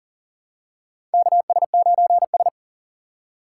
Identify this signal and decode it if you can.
Morse code, heard as a keyed tone, about 40 wpm, 700 Hz. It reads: KS9S
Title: USA Call Signs - 4 Characters - 4x Character Spacing - 40wpm